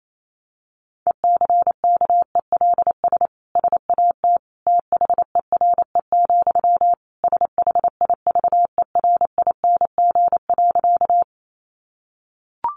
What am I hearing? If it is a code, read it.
Morse code, heard as a keyed tone, about 28 wpm, 700 Hz. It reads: ECKELS SAT THERE, SHIVERING.
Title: A Sound Of Thunder 28wpm